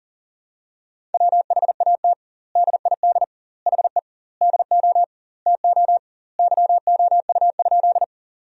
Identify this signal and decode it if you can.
Morse code, heard as a keyed tone, about 40 wpm, 700 Hz. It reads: WHAT DID HE DO TO YOU?
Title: 2x Word-Spacing - Sentences from Top 100 Words - 40wpm